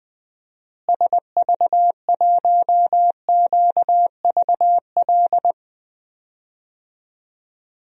Morse code, heard as a keyed tone, about 21 wpm, 700 Hz. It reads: SV1QVL
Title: DXCC-100 Call Signs - 6 Characters - Repeated 2x - 20wpm